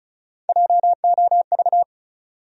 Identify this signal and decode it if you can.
Morse code, heard as a keyed tone, about 36 wpm, 700 Hz. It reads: JOV
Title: USA Call Sign Postfixes - 3 Letters 35wpm